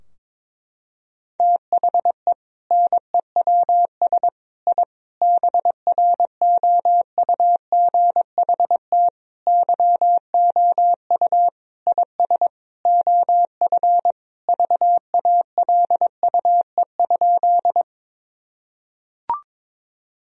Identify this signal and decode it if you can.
Morse code, heard as a keyed tone, about 22 wpm, 700 Hz. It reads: THE NEWS I BROUGHT YOU IS OF VALUE?
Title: Sentences from Top 1000 Words - Review Part 3 - 22wpm